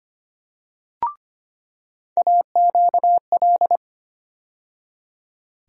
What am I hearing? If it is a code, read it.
Morse code, heard as a keyed tone, about 25 wpm, 700 Hz. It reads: AQL
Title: USA Call Sign Postfixes - 3 Letters 25wpm